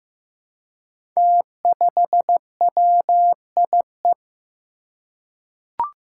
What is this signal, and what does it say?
Morse code, heard as a keyed tone, about 15 wpm, 700 Hz. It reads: T5WIE
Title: Five Challenging Letter-Number Combinations 15wpm